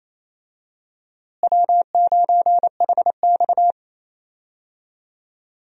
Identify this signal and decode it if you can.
Morse code, heard as a keyed tone, about 28 wpm, 700 Hz. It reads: W9HX
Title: USA Call Signs - 4 Characters - Repeated 2x - 28wpm